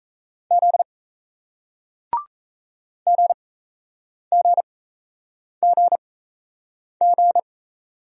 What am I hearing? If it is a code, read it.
Morse code, heard as a keyed tone, about 38 wpm, spaced as for 8 wpm, 700 Hz. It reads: Z GGGG